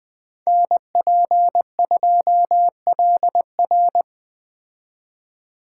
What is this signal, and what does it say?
Morse code, heard as a keyed tone, about 21 wpm, 700 Hz. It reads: NP2LR
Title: Speed Racing - USA 2x2 Call Signs 17wpm